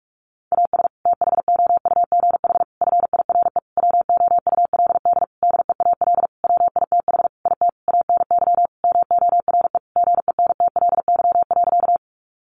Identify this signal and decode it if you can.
Morse code, heard as a keyed tone, about 44 wpm, 700 Hz. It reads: AS THOUGH FIRE WOULD DEAL WITH IT ANY MORE GENTLY.
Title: Sentences from Top 1500 Words 45wpm